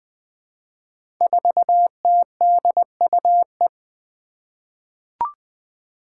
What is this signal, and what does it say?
Morse code, heard as a keyed tone, about 20 wpm, 700 Hz. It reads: 4TDUE
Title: Five Challenging Letter-Number Combinations 20wpm